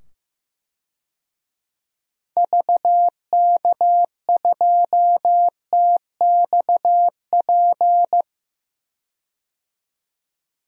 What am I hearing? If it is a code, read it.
Morse code, heard as a keyed tone, about 15 wpm, 700 Hz. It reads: VK2TXP
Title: DXCC-100 Call Signs - 6 Characters - 15wpm